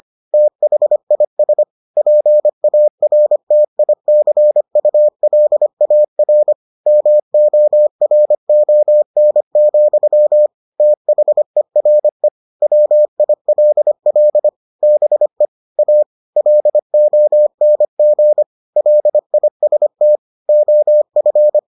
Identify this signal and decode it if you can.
Morse code, heard as a keyed tone, about 25 wpm, 600 Hz. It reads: THIS PARTICULAR MORON, THERE WILL BE A LONG LIST OF